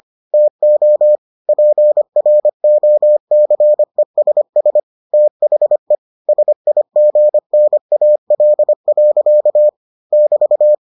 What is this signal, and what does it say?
Morse code, heard as a keyed tone, about 25 wpm, 600 Hz. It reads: TO PROCESS THE SIGNAL. <BT>